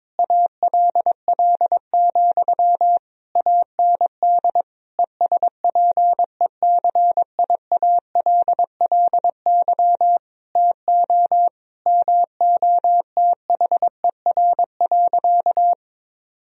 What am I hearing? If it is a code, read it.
Morse code, heard as a keyed tone, about 21 wpm, 700 Hz. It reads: ALL, AND ESPECIALLY TO MOTHER.
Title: Sentences from Top 1000 Words - Review Part 4 - 22wpm